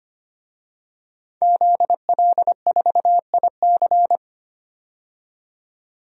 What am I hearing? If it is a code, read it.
Morse code, heard as a keyed tone, about 25 wpm, 700 Hz. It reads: ZL4IC
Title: DXCC-100 Call Signs - 5 Characters - 25wpm